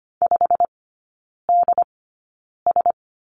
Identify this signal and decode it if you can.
Morse code, heard as a keyed tone, about 25 wpm, spaced as for 9 wpm, 700 Hz. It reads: 5DS
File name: Lesson 17e - Letter u - Three Characters 25wpm